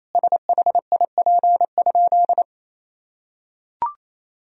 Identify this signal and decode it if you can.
Morse code, heard as a keyed tone, about 28 wpm, 700 Hz. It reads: SHIP?
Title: Sentences from Top 1000 Words - Review Part 2 - 28wpm